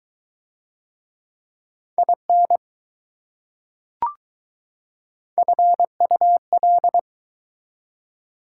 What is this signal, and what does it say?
Morse code, heard as a keyed tone, about 21 wpm, 700 Hz. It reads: IN FUL